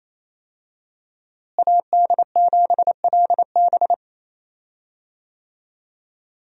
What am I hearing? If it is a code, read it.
Morse code, heard as a keyed tone, about 28 wpm, 700 Hz. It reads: AD7LB
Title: USA Call Signs - 5 Characters - Repeated 3x - 28wpm